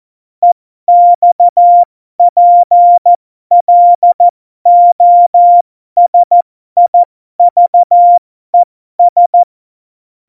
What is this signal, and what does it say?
Morse code, heard as a keyed tone, about 14 wpm, spaced as for 12 wpm, 700 Hz. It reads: EXPLOSIVES
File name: Morsekurs / Witze